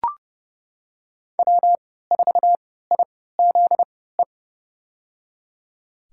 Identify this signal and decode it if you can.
Morse code, heard as a keyed tone, about 30 wpm, spaced as for 17 wpm, 700 Hz. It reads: W4IZE
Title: USA Call Signs - 5 Characters - 3x Character Spacing - 30wpm